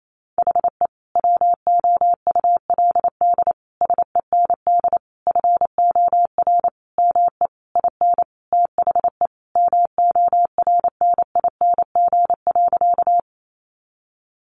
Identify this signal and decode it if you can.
Morse code, heard as a keyed tone, about 28 wpm, 700 Hz. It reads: HE WOULD SEND FOR ME IN THE MORNING.
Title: Sentences from Top 1500 Words 28wpm